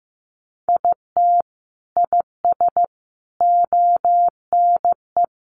Morse code, heard as a keyed tone, about 15 wpm, 700 Hz. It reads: IT IS ONE
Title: Sets of 5 Words 15wpm